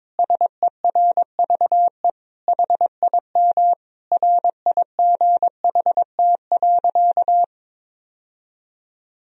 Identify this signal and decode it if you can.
Morse code, heard as a keyed tone, about 22 wpm, 700 Hz. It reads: SERVE HIM RIGHT.
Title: Sentences from Top 1000 Words - Review Part 2 - 22wpm